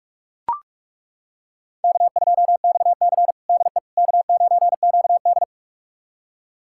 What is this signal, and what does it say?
Morse code, heard as a keyed tone, about 41 wpm, 700 Hz. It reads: K2XC DE K9QD
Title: USA Callsign DE Callsign - 4 Characters - 45wpm